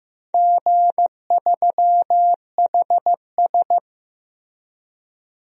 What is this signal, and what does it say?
Morse code, heard as a keyed tone, about 15 wpm, 700 Hz. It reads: G3HS
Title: DXCC-100 Call Signs - 4 Characters - Repeated 3x - 15wpm